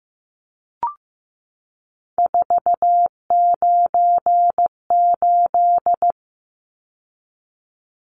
Morse code, heard as a keyed tone, about 15 wpm, 700 Hz. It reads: 498
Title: Numbers - 3-Digit 15wpm